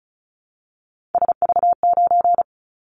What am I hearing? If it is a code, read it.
Morse code, heard as a keyed tone, about 36 wpm, 700 Hz. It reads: SV9
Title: Callsign Prefixes for Top 100 DXCC - 35wpm